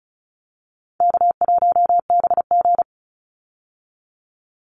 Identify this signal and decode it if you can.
Morse code, heard as a keyed tone, about 34 wpm, 700 Hz. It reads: K1BG
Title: Call Signs - 35wpm